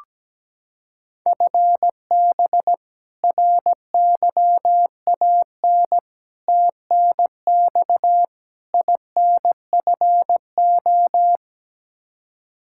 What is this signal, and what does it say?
Morse code, heard as a keyed tone, about 17 wpm, 700 Hz. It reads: FB RYAN TNX INFO
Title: Practice QSO - Exchange 3 - More Info - 17wpm